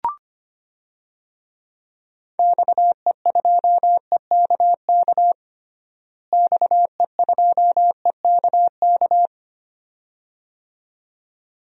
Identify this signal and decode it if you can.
Morse code, heard as a keyed tone, about 25 wpm, 700 Hz. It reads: XE2EKK XE2EKK
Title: DXCC-100 Call Signs - 6 Characters - Repeated 2x - 25wpm